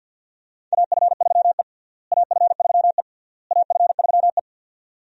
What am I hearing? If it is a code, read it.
Morse code, heard as a keyed tone, about 51 wpm, 700 Hz. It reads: AF3E AF3E AF3E
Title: USA Call Signs - 4 Characters - Repeated 3x - 50wpm